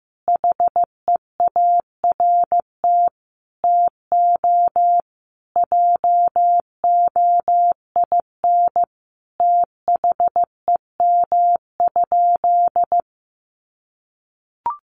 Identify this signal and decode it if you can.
Morse code, heard as a keyed tone, about 15 wpm, 700 Hz. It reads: HEART TO JOIN THEM?
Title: Sentences from Top 1400 Words 15wpm